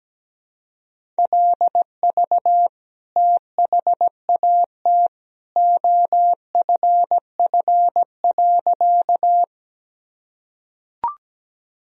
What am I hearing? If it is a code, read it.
Morse code, heard as a keyed tone, about 17 wpm, 700 Hz. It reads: LV THAT OFF.
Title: Sentences from Top 400 Words and Abbreviations 17wpm